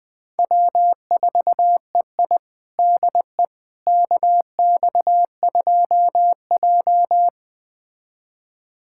Morse code, heard as a keyed tone, about 20 wpm, 700 Hz. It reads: W4EI DE KX2J
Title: USA Callsign DE Callsign - 4 Characters - 20wpm